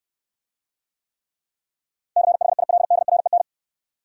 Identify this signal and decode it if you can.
Morse code, heard as a keyed tone, about 69 wpm, 690 Hz. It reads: CHILDREN